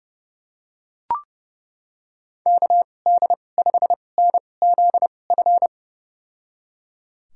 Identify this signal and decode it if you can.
Morse code, heard as a keyed tone, about 30 wpm, spaced as for 22 wpm, 700 Hz. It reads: KD5NZF
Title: USA Call Signs - 2x Character Spacing - 30wpm